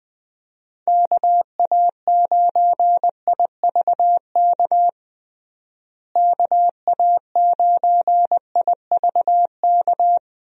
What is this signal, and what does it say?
Morse code, heard as a keyed tone, about 20 wpm, 700 Hz. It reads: KA9IVK KA9IVK
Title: USA Call Signs - 6 Characters - Repeated 2x - 20wpm